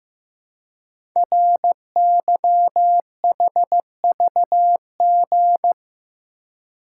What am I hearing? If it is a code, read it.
Morse code, heard as a keyed tone, about 15 wpm, 700 Hz. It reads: RYHVG
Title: Code Groups - 5 Letters - 15wpm